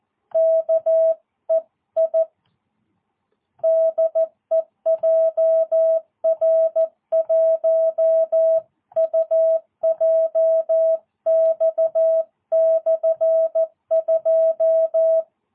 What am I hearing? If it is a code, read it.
Morse code, heard as a keyed tone, about 14 wpm, 650 Hz. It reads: KEI DE JR1UJX/2
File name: jh8cbh / 2022cw